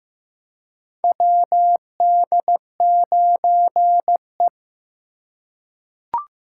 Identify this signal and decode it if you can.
Morse code, heard as a keyed tone, about 15 wpm, 700 Hz. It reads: WD9E
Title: USA Call Signs - 4 Characters - 2x Character Spacing - 15wpm